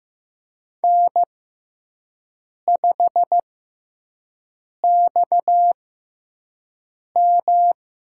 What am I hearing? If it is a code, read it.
Morse code, heard as a keyed tone, about 15 wpm, spaced as for 5 wpm, 700 Hz. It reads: N5XM
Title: USA Call Signs - 4 Characters - 6x Character Spacing - 15wpm